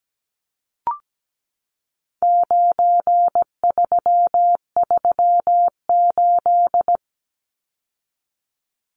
Morse code, heard as a keyed tone, about 17 wpm, 700 Hz. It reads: 9338
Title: Numbers - 4-Digit 17wpm